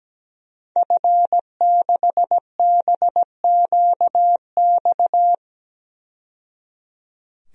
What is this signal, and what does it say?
Morse code, heard as a keyed tone, about 17 wpm, 700 Hz. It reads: F6BQX
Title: DXCC-100 Call Signs - 5 Characters - 17wpm